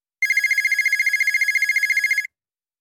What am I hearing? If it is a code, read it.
There is no code.